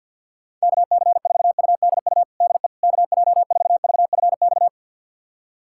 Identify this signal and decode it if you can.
Morse code, heard as a keyed tone, about 52 wpm, 700 Hz. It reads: KC4UDU DE KJ4VFX
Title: USA Callsign DE Callsign - 6 Characters - 50wpm